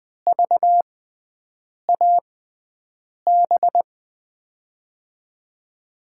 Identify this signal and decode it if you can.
Morse code, heard as a keyed tone, about 20 wpm, spaced as for 7 wpm, 700 Hz. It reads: VAB
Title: USA Call Signs - 5 Characters - 6x Character Spacing - 20wpm